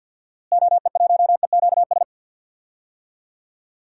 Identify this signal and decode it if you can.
Morse code, heard as a keyed tone, about 53 wpm, 700 Hz. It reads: OE1EQS
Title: DXCC-100 Call Signs - 6 Characters - Repeated 3x - 50wpm